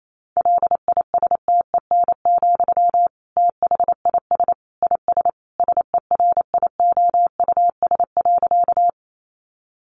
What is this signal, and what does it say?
Morse code, heard as a keyed tone, about 28 wpm, 700 Hz. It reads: LISTEN, THIS IS SERIOUS.
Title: Sentences from Top 1100 Words 28wpm